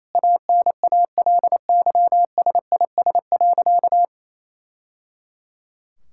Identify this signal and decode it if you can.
Morse code, heard as a keyed tone, about 28 wpm, 700 Hz. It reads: ANALYSIS.
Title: Sentences from Top 1000 Words - Review Part 5 - 28wpm